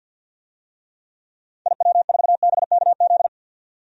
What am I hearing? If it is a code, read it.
Morse code, heard as a keyed tone, about 48 wpm, 700 Hz. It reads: IW4BKZ
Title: DXCC-100 Call Signs - 6 Characters - 50wpm